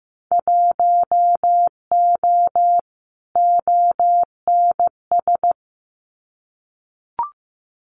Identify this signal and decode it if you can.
Morse code, heard as a keyed tone, about 15 wpm, 700 Hz. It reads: 1O ONS
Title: Winter Field Day Exchanges 15wpm